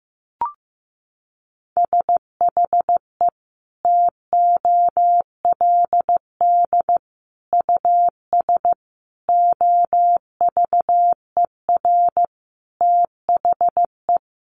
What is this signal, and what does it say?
Morse code, heard as a keyed tone, about 15 wpm, 700 Hz. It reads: SHE TOLD US OVER THE
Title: Sentences from Top 1100 Words 15wpm